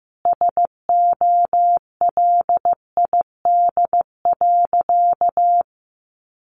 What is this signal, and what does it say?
Morse code, heard as a keyed tone, about 14 wpm, 700 Hz. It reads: SOLID.
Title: Sentences from Top 1500 Words 15wpm